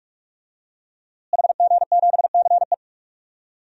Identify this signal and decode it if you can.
Morse code, heard as a keyed tone, about 46 wpm, 700 Hz. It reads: HG7CE
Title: DXCC-100 Call Signs - 5 Characters - Repeated 3x - 45wpm